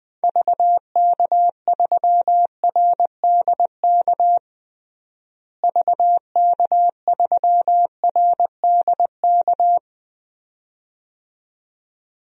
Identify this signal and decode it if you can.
Morse code, heard as a keyed tone, about 20 wpm, 700 Hz. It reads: VK3RDK VK3RDK
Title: DXCC-100 Call Signs - 6 Characters - Repeated 2x - 20wpm